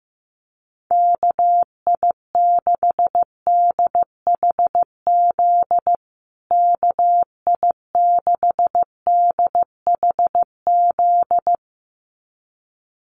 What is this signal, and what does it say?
Morse code, heard as a keyed tone, about 15 wpm, 700 Hz. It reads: KI6DHZ KI6DHZ
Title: USA Call Signs Repeated 2x 15wpm